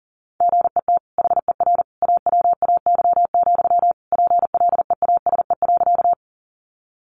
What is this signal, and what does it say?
Morse code, heard as a keyed tone, about 40 wpm, 700 Hz. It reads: GET HER AWAY, PLEASE.
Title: Sentences from Top 1500 Words 40wpm